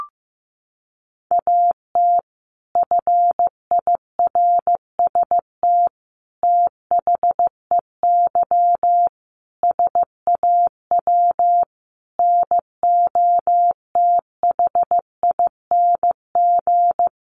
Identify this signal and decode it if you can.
Morse code, heard as a keyed tone, about 15 wpm, 700 Hz. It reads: AT FIRST THEY SAW NOTHING
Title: Sentences from Top 600 Words 15wpm